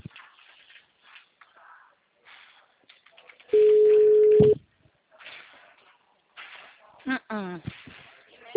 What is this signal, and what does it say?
Ringback tone (Europe, 425 Hz), with speech.